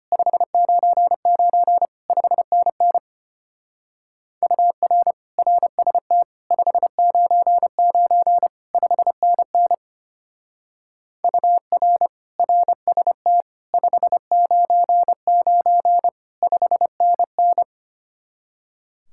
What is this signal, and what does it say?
Morse code, heard as a keyed tone, about 32 wpm, 700 Hz. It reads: 599 5NN UR RST 599 5NN UR RST 599 5NN